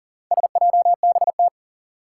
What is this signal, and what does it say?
Morse code, heard as a keyed tone, about 42 wpm, 700 Hz. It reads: SJBT
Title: Four Challenging Letter-Number Combinations 40wpm